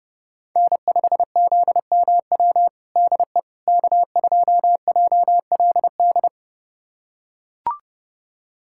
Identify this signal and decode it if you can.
Morse code, heard as a keyed tone, about 30 wpm, 700 Hz. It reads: N5ZMW DE K2JLD
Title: USA Callsign DE Callsign - 5 Characters - 30wpm